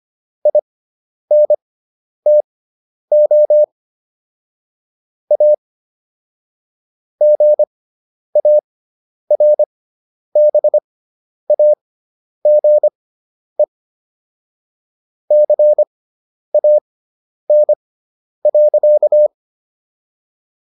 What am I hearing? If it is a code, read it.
Morse code, heard as a keyed tone, about 25 wpm, spaced as for 10 wpm, 600 Hz. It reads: INTO A GARBAGE CAN.